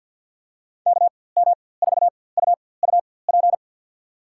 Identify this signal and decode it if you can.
Morse code, heard as a keyed tone, about 51 wpm, spaced as for 23 wpm, 700 Hz. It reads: KM4UUP
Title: USA Call Signs - 4x Character Spacing - 50wpm